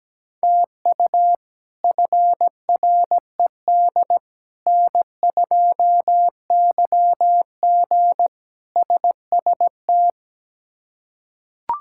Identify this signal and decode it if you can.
Morse code, heard as a keyed tone, about 17 wpm, 700 Hz. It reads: TU FRED N2YG SST
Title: K1USN SST QSO Practice - 17wpm